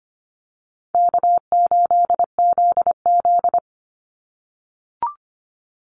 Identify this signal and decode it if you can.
Morse code, heard as a keyed tone, about 25 wpm, 700 Hz. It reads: K8ZZ
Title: Easy Call Signs - 25wpm